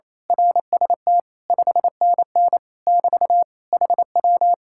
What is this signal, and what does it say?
Morse code, heard as a keyed tone, about 26 wpm, 700 Hz. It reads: RST 5NN <BT> HW
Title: Speed-Racing Common QSO Phrases Encore - 28wpm